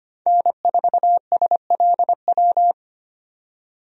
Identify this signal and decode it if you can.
Morse code, heard as a keyed tone, about 25 wpm, 700 Hz. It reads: N4SLW
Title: USA Call Signs - 5 Characters - 25wpm